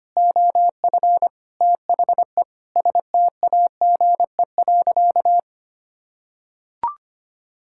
Morse code, heard as a keyed tone, about 25 wpm, 700 Hz. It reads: OF THE STAGE.
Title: Sentences from Top 1600 Words 25wpm